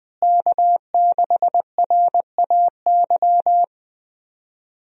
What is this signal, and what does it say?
Morse code, heard as a keyed tone, about 20 wpm, 700 Hz. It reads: K6RAY
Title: USA Call Signs - 5 Characters - Repeated 2x - 20wpm